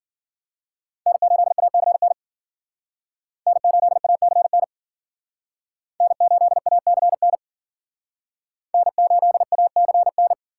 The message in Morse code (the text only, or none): N8ACN N8ACN N8ACN N8ACN